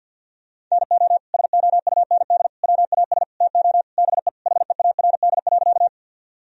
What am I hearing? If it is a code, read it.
Morse code, heard as a keyed tone, about 47 wpm, 700 Hz. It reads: NO SOUND WAS TO BE HEARD.
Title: Sentences from Top 1000 Words - Review Part 1 - 50wpm